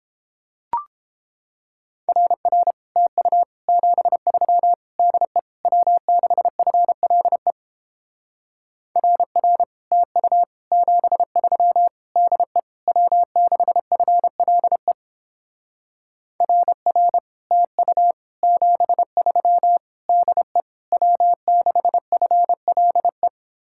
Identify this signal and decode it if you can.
Morse code, heard as a keyed tone, about 31 wpm, 700 Hz. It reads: RR TU 73 DE W6FLE RR TU 73 DE W6FLE RR TU 73 DE W6FLE